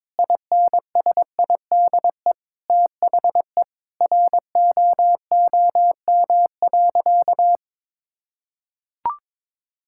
Morse code, heard as a keyed tone, about 22 wpm, 700 Hz. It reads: INSIDE THE ROOM.
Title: Sentences from Top 1000 Words - Review Part 6 - 22wpm